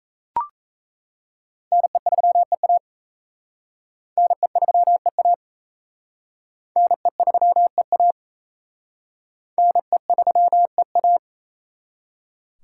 Morse code, heard as a keyed tone, about 35 wpm, 700 Hz. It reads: NE3EA NE3EA NE3EA NE3EA
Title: Speed Racing - USA Call Signs 28wpm